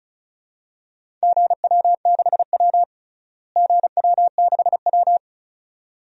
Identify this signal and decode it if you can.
Morse code, heard as a keyed tone, about 34 wpm, 700 Hz. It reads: GW6W GW6W
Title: DXCC-100 Call Signs - 4 Characters - Repeated 2x - 35wpm